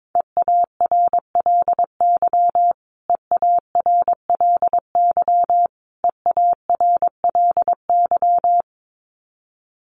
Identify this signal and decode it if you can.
Morse code, heard as a keyed tone, about 22 wpm, 700 Hz. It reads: EARLY EARLY EARLY
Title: Top 500 Words Repeated 22wpm